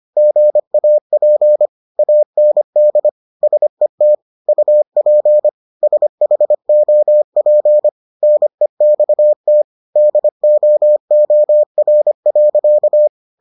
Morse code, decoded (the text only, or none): GAP AND SET UP SHOP NEXT DOOR.